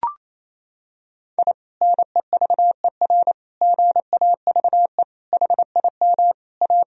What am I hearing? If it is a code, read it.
Morse code, heard as a keyed tone, about 28 wpm, 700 Hz. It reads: I NEVER GAVE HIM A
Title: Sentences from Top 1000 Words - Review Part 1 - 28wpm